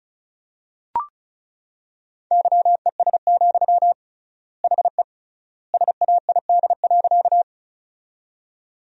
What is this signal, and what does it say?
Morse code, heard as a keyed tone, about 34 wpm, 700 Hz. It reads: YES, HE SAID.
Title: 3x Word-Spacing - Sentences from Top 500 Words - 35wpm